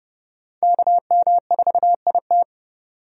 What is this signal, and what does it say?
Morse code, heard as a keyed tone, about 32 wpm, 700 Hz. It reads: KM4IT
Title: USA Call Signs - 5 Characters - 30wpm